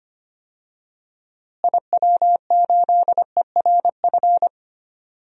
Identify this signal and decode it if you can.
Morse code, heard as a keyed tone, about 25 wpm, 700 Hz. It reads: IW8ERF
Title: DXCC-100 Call Signs - 6 Characters - Repeated 3x - 25wpm